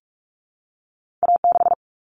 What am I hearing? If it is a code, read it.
Morse code, heard as a keyed tone, about 46 wpm, 700 Hz. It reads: A6
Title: Common Cut Numbers - 2-Digit 45wpm